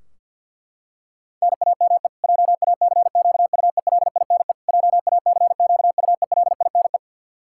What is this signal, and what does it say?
Morse code, heard as a keyed tone, about 47 wpm, 700 Hz. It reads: NAME JACQUELINE JACQUELINE
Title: K1USN SST QSO Practice 50wpm